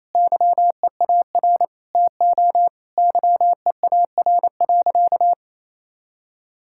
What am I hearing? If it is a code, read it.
Morse code, heard as a keyed tone, about 28 wpm, 700 Hz. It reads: YEAR TO YEAR.
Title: Sentences from Top 1600 Words 28wpm